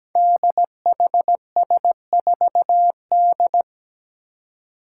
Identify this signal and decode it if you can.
Morse code, heard as a keyed tone, about 17 wpm, 700 Hz. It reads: DHS4D
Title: Five Challenging Letter-Number Combinations 17wpm